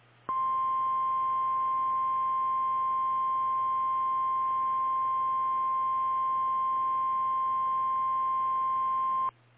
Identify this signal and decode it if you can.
Emergency Alert System (SAME) tones (the 1050 Hz attention tone). It is heard from 0.2 to 9.4 s.
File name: eas / KKPRKXPN / KKPRKXPN